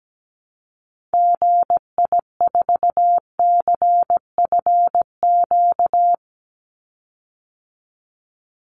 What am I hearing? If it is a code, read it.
Morse code, heard as a keyed tone, about 17 wpm, 700 Hz. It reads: GI4CFQ
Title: Call Signs Repeated 3x - 17wpm